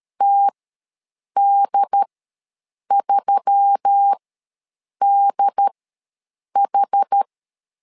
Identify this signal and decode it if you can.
Morse code, heard as a keyed tone, about 13 wpm, spaced as for 7 wpm, 800 Hz. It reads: TD3DH
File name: LEZIONI / 4 Q1S3 ws 12wpm 2038